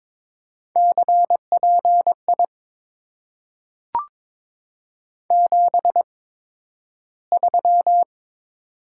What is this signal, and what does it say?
Morse code, heard as a keyed tone, about 22 wpm, spaced as for 12 wpm, 690 Hz. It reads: CPI 73
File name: Top 90 QSO Elements - 8x Character Spacing - 22wpm